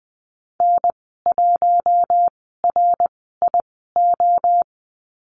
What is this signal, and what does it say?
Morse code, heard as a keyed tone, about 20 wpm, spaced as for 14 wpm, 700 Hz. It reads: N1RIO
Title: 069 Lesson 12g - Letter d - Call Signs 20wpm